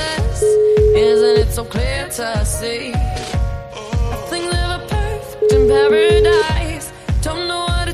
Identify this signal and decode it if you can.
Ringback tone (Europe, 425 Hz), with noise.